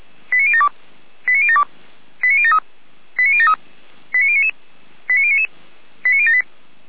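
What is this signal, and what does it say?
Five-tone selective call (ZVEI), heard as quick runs of five tones, tones of about 70 ms. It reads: 89072 89072 89073 89073 89099 89099 89078
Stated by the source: ZVEI five-tone selective call received on an SDR.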